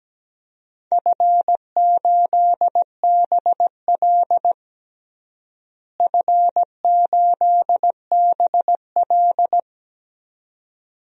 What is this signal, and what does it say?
Morse code, heard as a keyed tone, about 17 wpm, 700 Hz. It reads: F8BL F8BL
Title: DXCC-100 Call Signs - 4 Characters - Repeated 2x - 17wpm